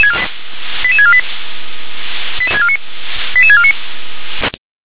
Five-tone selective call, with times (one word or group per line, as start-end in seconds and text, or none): 0.8-1.2 88648
3.3-3.7 88649